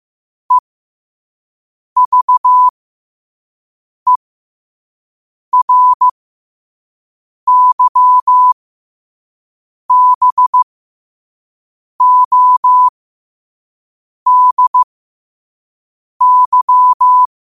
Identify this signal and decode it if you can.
Morse code, heard as a keyed tone, about 15 wpm, spaced as for 5 wpm, 1000 Hz. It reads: EVERYBODY
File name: qotdfiles / 5wpm